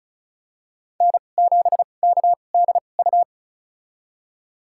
Morse code, heard as a keyed tone, about 34 wpm, spaced as for 25 wpm, 700 Hz. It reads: N7KDU
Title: USA Call Signs - 5 Characters - 2x Character Spacing - 35wpm